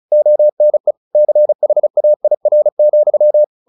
Morse code, heard as a keyed tone, about 34 wpm, 600 Hz. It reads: ONE CHAIR,
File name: the old man and the sea / the old man and the sea 1 35wpm